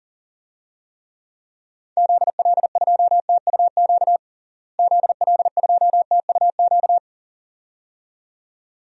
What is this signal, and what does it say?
Morse code, heard as a keyed tone, about 40 wpm, 700 Hz. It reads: ZL2TUQ ZL2TUQ
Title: DXCC-100 Call Signs - 6 Characters - Repeated 2x - 40wpm